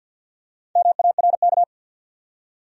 Morse code, heard as a keyed tone, about 53 wpm, 700 Hz. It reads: MARK
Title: USA Common Names - Max 4 Letters - 50wpm